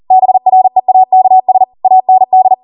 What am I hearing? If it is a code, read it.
Morse code, heard as a keyed tone, about 40 wpm, 750 Hz. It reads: BREAKS AND